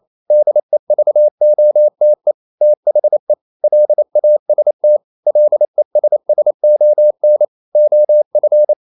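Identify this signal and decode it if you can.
Morse code, heard as a keyed tone, about 28 wpm, 600 Hz. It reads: DEVOTE THE LAST LESSON OF
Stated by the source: Morse code practice at 28 wpm, text of Ken Follett's novel Never in English, from CW med Gnister.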